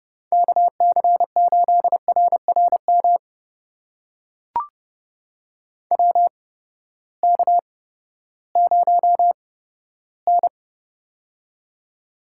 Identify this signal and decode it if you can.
Morse code, heard as a keyed tone, about 30 wpm, spaced as for 15 wpm, 700 Hz. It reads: KC8RRM WK0N
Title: USA Call Signs - 8x Character Spacing - 30wpm